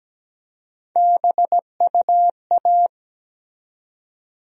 Morse code, heard as a keyed tone, about 17 wpm, 700 Hz. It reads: BUA
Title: USA Call Sign Postfixes - 3 Letters 17wpm